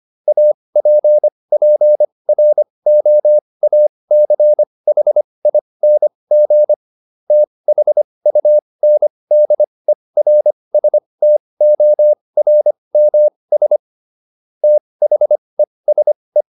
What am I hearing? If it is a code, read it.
Morse code, heard as a keyed tone, about 25 wpm, spaced as for 20 wpm, 600 Hz. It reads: APPROACHING THUNDERSTORMS THESE